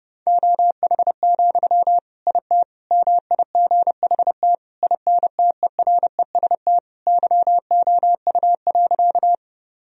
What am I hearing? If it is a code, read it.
Morse code, heard as a keyed tone, about 30 wpm, 700 Hz. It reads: OH, IT MIGHT INTEREST YOU.
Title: Sentences from Top 1000 Words - Review Part 2 - 30wpm